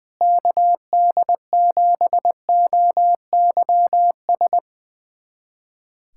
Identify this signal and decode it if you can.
Morse code, heard as a keyed tone, about 21 wpm, 700 Hz. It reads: KD7OYS
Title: USA Callsign DE Callsign - 6 Characters - 20wpm